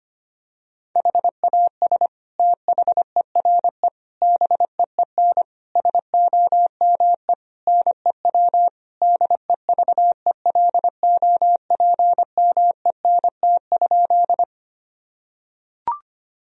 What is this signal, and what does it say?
Morse code, heard as a keyed tone, about 25 wpm, 700 Hz. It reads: HAS THERE BEEN SOME NEW DEVELOPMENT?
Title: Sentences from Top 1000 Words - Review Part 4 - 25wpm